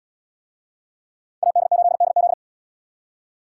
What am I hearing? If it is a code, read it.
Morse code, heard as a keyed tone, about 77 wpm, 700 Hz. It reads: AD8DZ